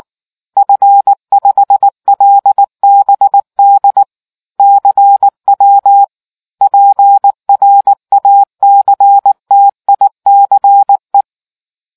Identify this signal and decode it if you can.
Morse code, heard as a keyed tone, about 19 wpm, 800 Hz. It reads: F5LBD CW PRACTICE